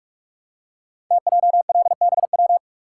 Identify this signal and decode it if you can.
Morse code, heard as a keyed tone, about 44 wpm, 690 Hz. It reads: TJLBW
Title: Code Groups - 5 Letter:Numbers - 45wpm